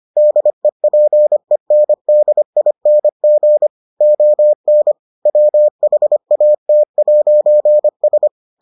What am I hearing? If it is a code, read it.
Morse code, heard as a keyed tone, about 25 wpm, 600 Hz. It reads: DEPENDING ON WHAT'S